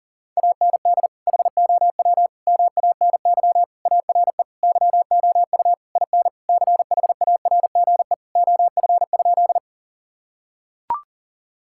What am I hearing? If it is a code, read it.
Morse code, heard as a keyed tone, about 40 wpm, 700 Hz. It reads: AND HOW MANY ARE YOU IN CHARGE OF?
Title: Sentences from Top 1000 Words - Review Part 6 - 40wpm